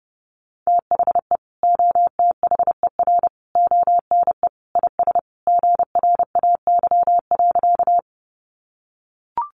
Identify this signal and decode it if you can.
Morse code, heard as a keyed tone, about 30 wpm, 700 Hz. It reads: THE OTHER ONE IS GRAY.
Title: Additional Easy Phrases 30wpm